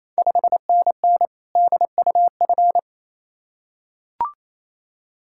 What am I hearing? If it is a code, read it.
Morse code, heard as a keyed tone, about 27 wpm, 690 Hz. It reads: 5NN DUF
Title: Ontario QSO Party Exchange 28wpm